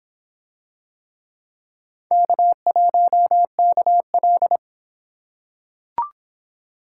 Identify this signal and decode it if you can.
Morse code, heard as a keyed tone, about 25 wpm, 700 Hz. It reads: K1KL